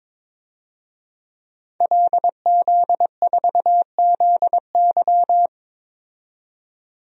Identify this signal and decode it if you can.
Morse code, heard as a keyed tone, about 22 wpm, 700 Hz. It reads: LZ4ZY
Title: DXCC-100 Call Signs - 5 Characters - 22wpm